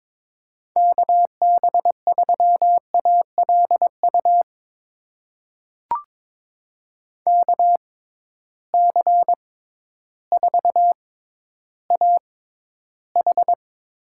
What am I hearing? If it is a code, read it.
Morse code, heard as a keyed tone, about 22 wpm, spaced as for 12 wpm, 700 Hz. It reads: KB3ALU KC4AH
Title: USA Call Signs - 6x Character Spacing - 22wpm